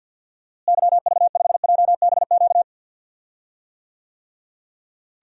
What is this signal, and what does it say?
Morse code, heard as a keyed tone, about 50 wpm, 700 Hz. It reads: YV5JBQ
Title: DXCC-100 Call Signs - 6 Characters - Repeated 2x - 50wpm